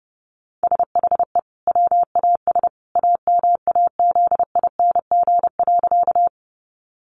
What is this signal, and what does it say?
Morse code, heard as a keyed tone, about 30 wpm, 700 Hz. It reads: SHE WAS AMAZING.
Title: Sentences from Top 1500 Words 30wpm